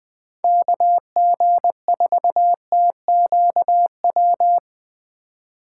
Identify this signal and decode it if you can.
Morse code, heard as a keyed tone, about 20 wpm, 700 Hz. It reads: KG4TQW
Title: Speed Racing - USA 2x3 Call Signs 17wpm